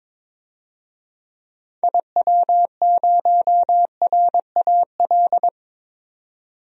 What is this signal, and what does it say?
Morse code, heard as a keyed tone, about 21 wpm, 700 Hz. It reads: IW0RAL
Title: DXCC-100 Call Signs - 6 Characters - 22wpm